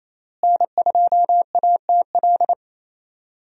Morse code, heard as a keyed tone, about 28 wpm, 700 Hz. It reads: N2ATL
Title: USA Call Signs - 5 Characters - Repeated 3x - 28wpm